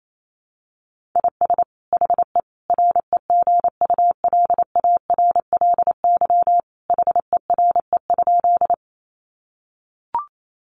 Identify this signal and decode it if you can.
Morse code, heard as a keyed tone, about 28 wpm, 700 Hz. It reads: IS HE REGULARLY HERE?
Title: Sentences from Top 1400 Words 28wpm